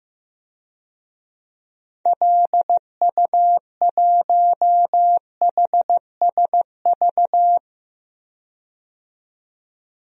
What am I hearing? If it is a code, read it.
Morse code, heard as a keyed tone, about 15 wpm, 700 Hz. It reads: LU1HSV
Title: DXCC-100 Call Signs - 6 Characters - 15wpm